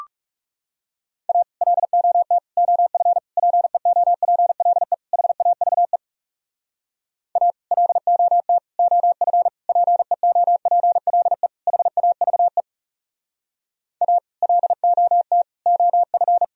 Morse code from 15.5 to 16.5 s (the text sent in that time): OF